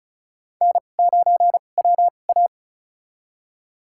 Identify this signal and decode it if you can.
Morse code, heard as a keyed tone, about 36 wpm, spaced as for 25 wpm, 700 Hz. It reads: N9WA
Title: USA Call Signs - 4 Characters - 2x Character Spacing - 35wpm